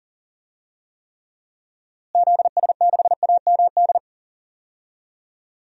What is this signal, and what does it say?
Morse code, heard as a keyed tone, about 40 wpm, 700 Hz. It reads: ZS6AMD
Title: DXCC-100 Call Signs - 6 Characters - Repeated 2x - 40wpm